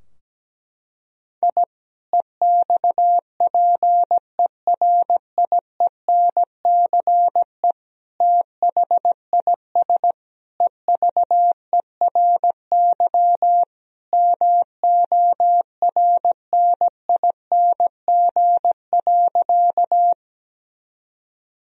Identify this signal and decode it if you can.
Morse code, heard as a keyed tone, about 17 wpm, 700 Hz. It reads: I EXPERIENCE THIS EVERY MORNING.
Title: Sentences from Top 1000 Words - Review Part 3 - 17wpm